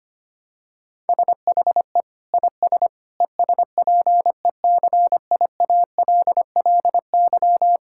Morse code, heard as a keyed tone, about 25 wpm, 700 Hz. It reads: SHE IS ESPECIALLY
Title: Sentences from Top 1000 Words - Review Part 6 - 25wpm